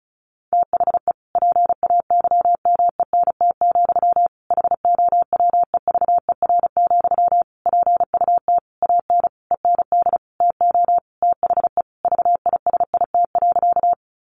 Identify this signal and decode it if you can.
Morse code, heard as a keyed tone, about 34 wpm, 700 Hz. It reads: THE PAYMENT, HOWEVER, PUT AN END TO THE VISIT.
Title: Sentences from Top 1200 Words 35wpm